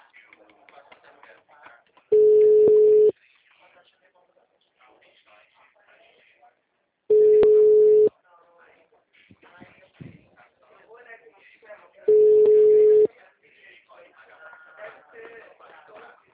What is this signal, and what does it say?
Ringback tone (Europe, 425 Hz), with speech.